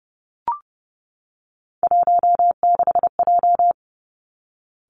Morse code, heard as a keyed tone, about 30 wpm, 700 Hz. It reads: 16J
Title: Three Challenging Letter-Number Combinations 30wpm